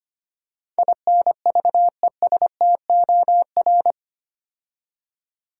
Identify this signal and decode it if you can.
Morse code, heard as a keyed tone, about 25 wpm, 700 Hz. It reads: INVESTOR
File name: 8 Letter Words - First 3 Letters Given - 25wpm